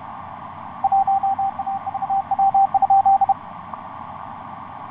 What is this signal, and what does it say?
Morse code, heard as a keyed tone, about 32 wpm, 780 Hz. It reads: 1AVW?
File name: xv9dx ik1avw etc 12m cw 16feb2010 0909